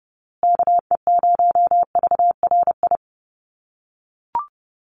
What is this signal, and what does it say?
Morse code, heard as a keyed tone, about 30 wpm, 700 Hz. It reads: KE0VRI
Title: USA Call Signs Repeated 2x 30wpm